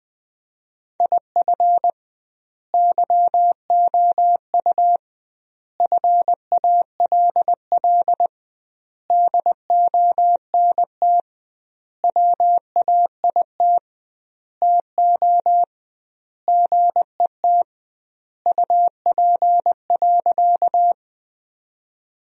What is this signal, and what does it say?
Morse code, heard as a keyed tone, about 20 wpm, 700 Hz. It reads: IF YOU FALL DONT WAIT TO GET UP.
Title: 2x Word-Spacing - Sentences from Top 500 Words - 20wpm